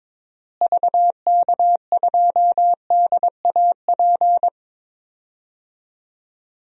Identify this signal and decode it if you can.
Morse code, heard as a keyed tone, about 22 wpm, 700 Hz. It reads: VK2DAP
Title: DXCC-100 Call Signs - 6 Characters - Repeated 3x - 22wpm